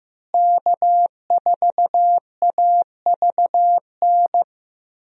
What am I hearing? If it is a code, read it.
Morse code, heard as a keyed tone, about 15 wpm, 700 Hz. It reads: K4AVN
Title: USA Call Signs - 5 Characters - Repeated 3x - 15wpm